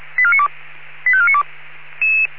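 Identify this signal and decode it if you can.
Five-tone selective call (ZVEI), heard as quick runs of five tones, tones of about 70 ms. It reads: 85492
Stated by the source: ZVEI five-tone selective call received by SDR.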